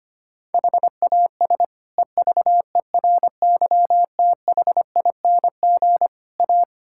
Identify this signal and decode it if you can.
Morse code, heard as a keyed tone, about 25 wpm, 700 Hz. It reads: HAS EVERYTHING A
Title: Sentences from Top 1600 Words 25wpm